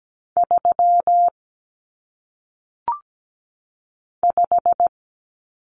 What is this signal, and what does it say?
Morse code, heard as a keyed tone, about 17 wpm, 700 Hz. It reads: 3 5
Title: Numbers - 1-Digit 17wpm